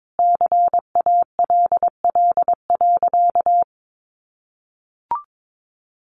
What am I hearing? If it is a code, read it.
Morse code, heard as a keyed tone, about 22 wpm, 700 Hz. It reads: CALL.
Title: Sentences from Top 1300 Words 22wpm